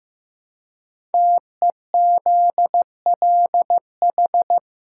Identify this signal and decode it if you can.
Morse code, heard as a keyed tone, about 15 wpm, 700 Hz. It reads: TEZLH
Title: Code Groups - 5 Letters - 15wpm